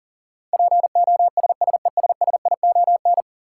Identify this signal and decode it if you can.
Morse code, heard as a keyed tone, about 40 wpm, 700 Hz. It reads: POSSESSION
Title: Sentences from Top 1400 Words - New Word Drill - 40wpm